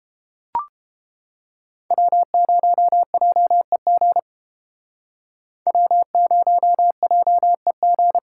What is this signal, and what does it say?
Morse code, heard as a keyed tone, about 32 wpm, 700 Hz. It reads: W0JEG W0JEG